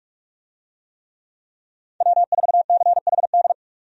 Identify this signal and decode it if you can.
Morse code, heard as a keyed tone, about 46 wpm, 700 Hz. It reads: W4CHD
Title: USA Call Signs - 5 Characters - 45wpm